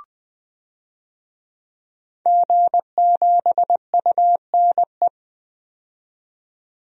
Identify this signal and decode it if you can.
Morse code, heard as a keyed tone, about 20 wpm, 700 Hz. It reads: G7UNE
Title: DXCC-100 Call Signs - 5 Characters - 20wpm